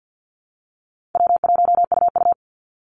Morse code, heard as a keyed tone, about 50 wpm, 700 Hz. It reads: W1UU
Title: Call Signs - 50wpm